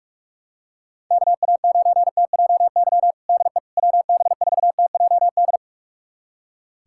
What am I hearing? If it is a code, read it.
Morse code, heard as a keyed tone, about 46 wpm, 700 Hz. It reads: KA9TJY DE WB4TJD
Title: USA Callsign DE Callsign 45wpm